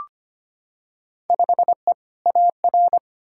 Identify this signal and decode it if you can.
Morse code, heard as a keyed tone, about 25 wpm, 700 Hz. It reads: 5E AR
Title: Field Day Exchanges Encore - 25wpm